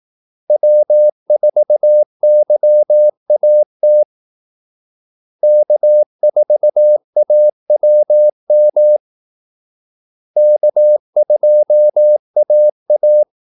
Morse code, heard as a keyed tone, about 18 wpm, 600 Hz. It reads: W4YAT K4AWM K2AA
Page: Kaldesignaler 18 wpm